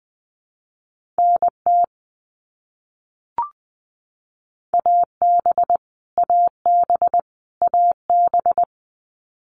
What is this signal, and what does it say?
Morse code, heard as a keyed tone, about 19 wpm, 700 Hz. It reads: NT AB AB AB
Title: Canadian Provinces and Territories - Repeated 3x - 20wpm